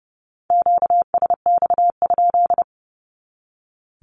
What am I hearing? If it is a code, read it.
Morse code, heard as a keyed tone, about 30 wpm, 700 Hz. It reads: QSX?
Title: Q-Codes 30wpm